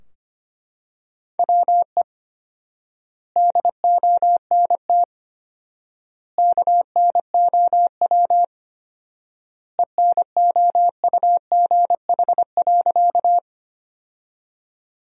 Morse code, heard as a keyed tone, about 25 wpm, 700 Hz. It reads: WE DONT KNOW ENOUGH.
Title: 4x Word-Spacing - Sentences from Top 400 Words - 25wpm